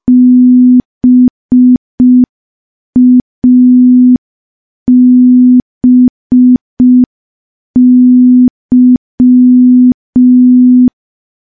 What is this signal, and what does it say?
Morse code, heard as a keyed tone, about 5 wpm, 250 Hz. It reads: BABY
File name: morse / morse-training-05WPM